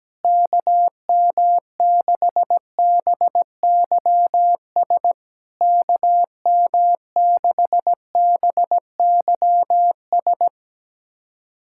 Morse code, heard as a keyed tone, about 17 wpm, 700 Hz. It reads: KM6BYS KM6BYS
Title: SOTA QSO Practice 17wpm